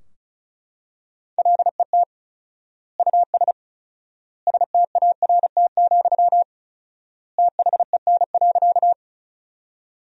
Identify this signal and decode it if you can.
Morse code, heard as a keyed tone, about 36 wpm, 700 Hz. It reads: LET US START, THEN.
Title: 4x Word-Spacing - Sentences from Top 300 Words - 35wpm